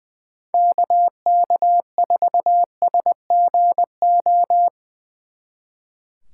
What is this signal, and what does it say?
Morse code, heard as a keyed tone, about 22 wpm, 700 Hz. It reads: KK4SGO
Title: Speed Racing - USA 2x3 Call Signs 20wpm